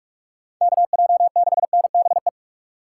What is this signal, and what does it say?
Morse code, heard as a keyed tone, about 58 wpm, 700 Hz. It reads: KJ6NBE